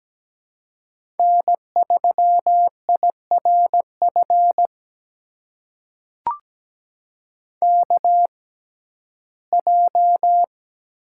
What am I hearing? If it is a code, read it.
Morse code, heard as a keyed tone, about 17 wpm, spaced as for 12 wpm, 700 Hz. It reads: N3IRF KJ
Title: USA Call Signs - 6x Character Spacing - 17wpm